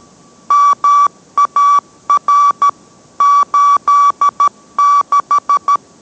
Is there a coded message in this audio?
Yes: Morse code from 0.5 to 5.8 s, heard as a keyed tone, about 13 wpm, 1150 Hz. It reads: MAR 86